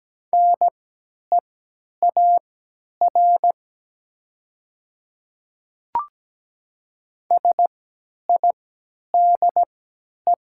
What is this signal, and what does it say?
Morse code, heard as a keyed tone, about 17 wpm, spaced as for 10 wpm, 700 Hz. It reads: NEAR SIDE
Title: 3x Character Spacing - Top 200 Words - 17wpm